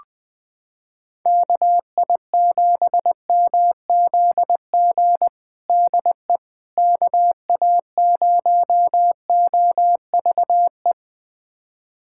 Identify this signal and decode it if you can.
Morse code, heard as a keyed tone, about 21 wpm, 690 Hz. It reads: KI7MZG DE KA0OVE
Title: USA Callsign DE Callsign 20wpm